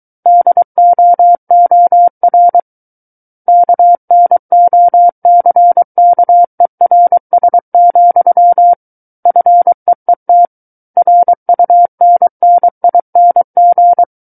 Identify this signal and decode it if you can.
Morse code, heard as a keyed tone, about 23 wpm, 700 Hz. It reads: DOOR KNOCKERS, FEET RUNNING
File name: Book One / 14-Chapter 14 - 23 WPM